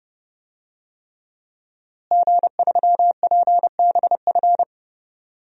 Morse code, heard as a keyed tone, about 30 wpm, 700 Hz. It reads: G3PBF
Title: DXCC-100 Call Signs - 5 Characters - Repeated 3x - 30wpm